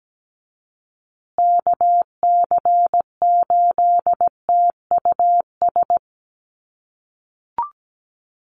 Morse code, heard as a keyed tone, about 17 wpm, 700 Hz. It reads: KC8TUS
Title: USA Call Signs Repeated 2x 17wpm